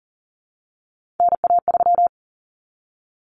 Morse code, heard as a keyed tone, about 40 wpm, 700 Hz. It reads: NA3
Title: Common Cut Numbers - 3-Digit 40wpm